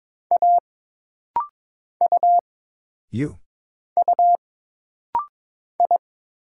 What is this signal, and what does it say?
Morse code, heard as a keyed tone, about 22 wpm, 700 Hz. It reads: A U U I
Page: This rapid-fire timing is geared to push your ICR (Instant Character Recognition) skills.